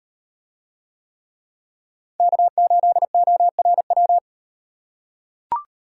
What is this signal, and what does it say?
Morse code, heard as a keyed tone, about 38 wpm, 700 Hz. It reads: K8ORW